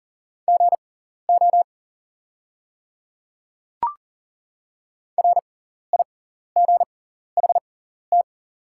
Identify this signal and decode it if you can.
Morse code, heard as a keyed tone, about 40 wpm, spaced as for 14 wpm, 690 Hz. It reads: GO RIGHT
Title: 6x Character Spacing - Top 200 Words - 40wpm